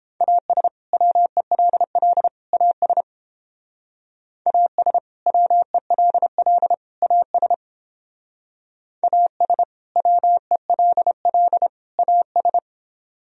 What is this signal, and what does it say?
Morse code, heard as a keyed tone, about 29 wpm, 700 Hz. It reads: AS WELL AS AS WELL AS AS WELL AS